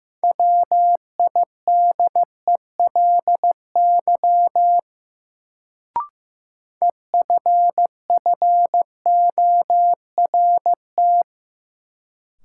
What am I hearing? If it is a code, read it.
Morse code, heard as a keyed tone, about 15 wpm, 700 Hz. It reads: WIDELY EFFORT
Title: Sentences from Top 1200 Words - New Word Drill Alt - 15wpm